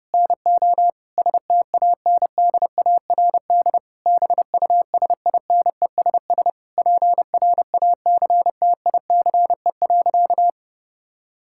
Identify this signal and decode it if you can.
Morse code, heard as a keyed tone, about 30 wpm, 700 Hz. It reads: NO STANDARD BUSINESS PRACTICE.
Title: Sentences from Top 1000 Words - Review Part 6 - 30wpm